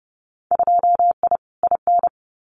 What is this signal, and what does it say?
Morse code, heard as a keyed tone, about 30 wpm, 700 Hz. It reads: 2I IN
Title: Winter Field Day Exchanges Encore 30wpm